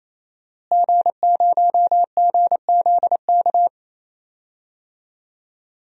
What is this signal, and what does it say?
Morse code, heard as a keyed tone, about 28 wpm, 700 Hz. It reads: G0GZK
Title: DXCC-100 Call Signs - 5 Characters - Repeated 3x - 28wpm